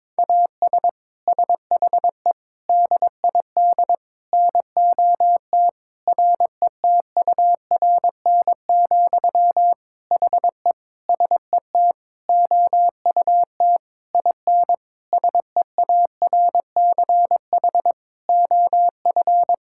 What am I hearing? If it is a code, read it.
Morse code, heard as a keyed tone, about 22 wpm, 700 Hz. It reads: AS SHE DID NOT RETURN, HE SET OUT IN SEARCH OF
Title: Sentences from Top 1000 Words - Review Part 5 - 22wpm